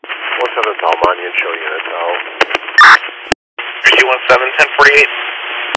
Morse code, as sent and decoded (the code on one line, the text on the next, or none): none